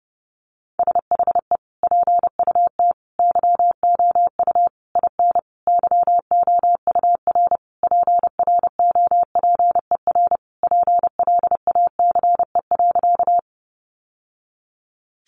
Morse code, heard as a keyed tone, about 30 wpm, 700 Hz. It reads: SHE PUT YOU IN YOUR PROPER PLACE.
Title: Sentences from Top 800 Words 30wpm